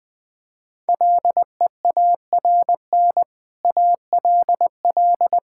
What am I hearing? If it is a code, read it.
Morse code, heard as a keyed tone, about 21 wpm, 700 Hz. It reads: LEARN ALL
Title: Sentences from Top 1000 Words - Review Part 5 - 20wpm